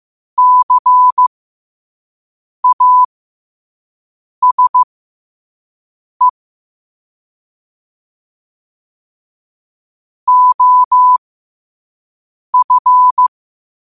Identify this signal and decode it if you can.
Morse code, heard as a keyed tone, about 15 wpm, spaced as for 5 wpm, 1000 Hz. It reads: CASE OF